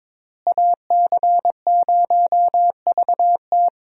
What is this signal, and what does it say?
Morse code, heard as a keyed tone, about 22 wpm, 700 Hz. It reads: AC0VT
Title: USA Call Signs - 5 Characters - Repeated 2x - 22wpm